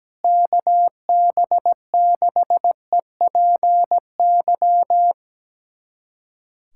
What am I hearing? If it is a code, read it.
Morse code, heard as a keyed tone, about 17 wpm, 700 Hz. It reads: KB6EPY
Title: USA Callsign DE Callsign 17wpm